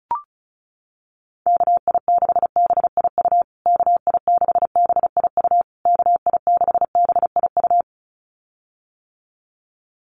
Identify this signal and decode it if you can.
Morse code, heard as a keyed tone, about 34 wpm, 700 Hz. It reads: KI6BIU KI6BIU KI6BIU
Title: USA Call Signs Repeated 3x 35wpm